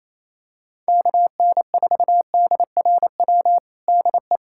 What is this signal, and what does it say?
Morse code, heard as a keyed tone, about 28 wpm, 700 Hz. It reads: KN4DRW DE
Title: USA Callsign DE Callsign 28wpm